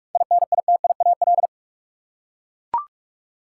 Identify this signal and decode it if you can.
Morse code, heard as a keyed tone, about 44 wpm, 700 Hz. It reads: INITIAL
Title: Seven Letter Words - 45wpm